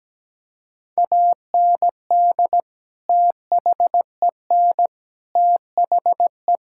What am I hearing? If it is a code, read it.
Morse code, heard as a keyed tone, about 17 wpm, 700 Hz. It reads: AND THEN THE
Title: Sentences from Top 1000 Words - Review Part 6 - 17wpm